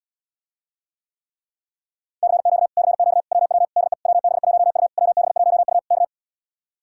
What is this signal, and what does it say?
Morse code, heard as a keyed tone, about 77 wpm, 700 Hz. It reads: CQ CQ WW DE WB1U WB1U K